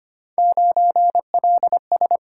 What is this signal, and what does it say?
Morse code, heard as a keyed tone, about 25 wpm, 700 Hz. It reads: 9LS
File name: Three Letter-Number Combinations - 25wpm